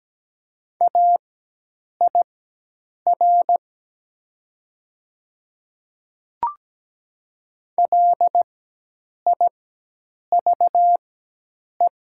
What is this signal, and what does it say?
Morse code, heard as a keyed tone, about 17 wpm, spaced as for 8 wpm, 700 Hz. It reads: AIR LIVE
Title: 4x Character Spacing - Top 200 Words - 17wpm